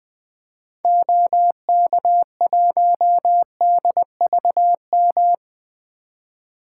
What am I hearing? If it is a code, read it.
Morse code, heard as a keyed tone, about 21 wpm, 700 Hz. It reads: OK1DVM
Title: 2023 SOTA Activator Call Signs - 20wpm